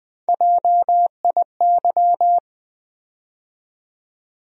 Morse code, heard as a keyed tone, about 21 wpm, 700 Hz. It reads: JIY
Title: USA Call Sign Postfixes - 3 Letters 20wpm